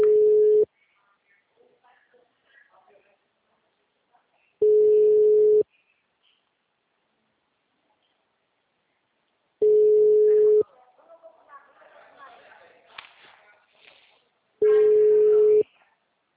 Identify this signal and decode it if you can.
Ringback tone (Europe, 425 Hz), with speech.
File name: out- -2014- - - .96758